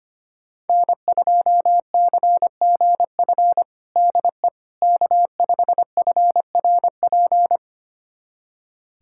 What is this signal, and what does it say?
Morse code, heard as a keyed tone, about 25 wpm, 700 Hz. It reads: N2CGF DE K5FRP
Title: USA Callsign DE Callsign - 5 Characters - 25wpm